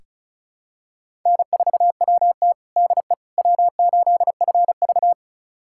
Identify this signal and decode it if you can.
Morse code, heard as a keyed tone, about 36 wpm, 700 Hz. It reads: N4WT DE W8FV
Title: USA Callsign DE Callsign - 4 Characters - 35wpm